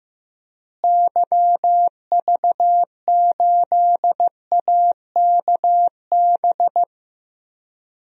Morse code, heard as a keyed tone, about 15 wpm, 700 Hz. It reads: YV8AKB
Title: DXCC-100 Call Signs - 6 Characters - Repeated 3x - 15wpm